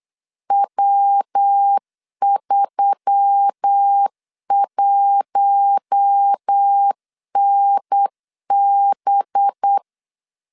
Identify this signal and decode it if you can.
Morse code, heard as a keyed tone, about 8 wpm, 800 Hz. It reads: W31NB
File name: LEZIONI / 8 YJ45 / 8 YJ45 8wpm 3214